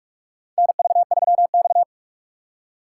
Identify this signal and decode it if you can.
Morse code, heard as a keyed tone, about 46 wpm, 700 Hz. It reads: NV3X
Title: USA Call Signs - 4 Characters - 3x Character Spacing - 45wpm